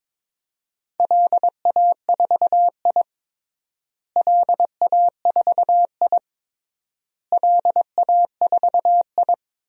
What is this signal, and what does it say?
Morse code, heard as a keyed tone, about 22 wpm, 700 Hz. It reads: LA4I LA4I LA4I
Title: DXCC-100 Call Signs - 4 Characters - Repeated 3x - 22wpm